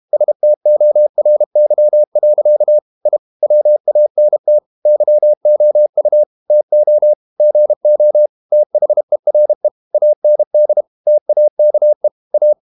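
Morse code, heard as a keyed tone, about 32 wpm, 600 Hz. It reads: STORY. I WANT YOU TO GO THERE AND TAKE A
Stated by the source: Morse code practice at 32 wpm from CW med Gnister, reading Ken Follett's novel Never in English.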